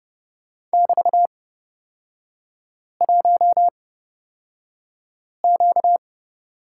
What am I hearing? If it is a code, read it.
Morse code, heard as a keyed tone, about 31 wpm, spaced as for 5 wpm, 690 Hz. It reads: <BT>1Q
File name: course / 30:10wpm / 236 Lesson 38e - Letter q - Three Characters 30wpm